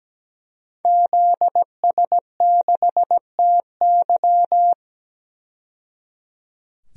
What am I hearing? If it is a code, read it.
Morse code, heard as a keyed tone, about 17 wpm, 700 Hz. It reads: ZS6TY
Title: DXCC-100 Call Signs - 5 Characters - 17wpm